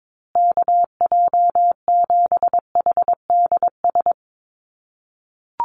Morse code, heard as a keyed tone, about 22 wpm, 700 Hz. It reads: KJ7HDS
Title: USA Call Signs Repeated 3x 22wpm